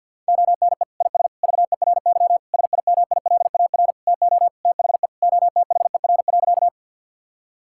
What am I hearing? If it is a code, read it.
Morse code, heard as a keyed tone, about 50 wpm, 700 Hz. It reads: ONE IS VERY SIMILAR TO THE OTHER.
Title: Sentences from Top 1000 Words - Review Part 5 - 50wpm